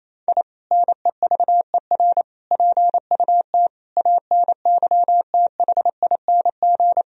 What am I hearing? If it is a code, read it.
Morse code, heard as a keyed tone, about 28 wpm, 700 Hz. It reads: I NEVER PUT ANYTHING
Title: Sentences from Top 1000 Words - Review Part 1 - 28wpm